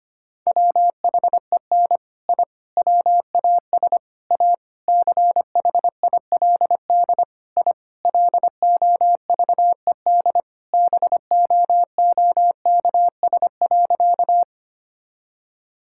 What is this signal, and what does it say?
Morse code, heard as a keyed tone, about 25 wpm, 700 Hz. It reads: WHEN I WAS A CHILD I LOVED BOOKS.
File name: practice / 25wpm / Sentences with 3-Word Phrases - 25wpm